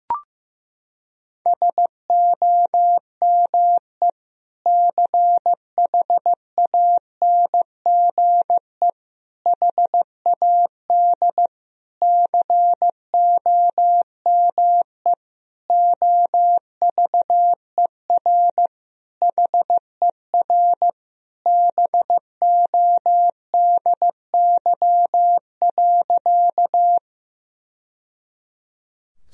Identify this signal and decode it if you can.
Morse code, heard as a keyed tone, about 15 wpm, 700 Hz. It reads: SOME CHANGE HAD COME OVER HER BODY.
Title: Sentences from Top 1000 Words - Review Part 2 - 15wpm